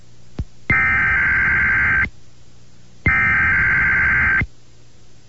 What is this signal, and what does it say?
Emergency Alert System (SAME) tones (header bursts), heard as harsh buzzing bursts. It is heard from 0.7 to 4.4 s.